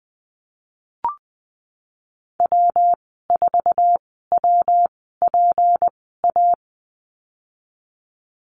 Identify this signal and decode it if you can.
Morse code, heard as a keyed tone, about 20 wpm, spaced as for 15 wpm, 700 Hz. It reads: W4WPA
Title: 167 Lesson 27g - Number 7 - Call Signs 20wpm